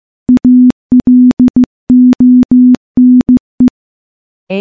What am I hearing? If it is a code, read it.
Morse code, heard as a keyed tone, about 15 wpm, 250 Hz. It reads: ALONE
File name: morse / morse-training-15WPM